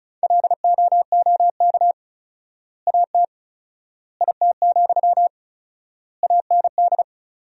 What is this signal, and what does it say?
Morse code, heard as a keyed tone, about 36 wpm, 700 Hz. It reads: LOOK AT IT, AND
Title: 4x Word-Spacing - Sentences from Top 200 Words - 35wpm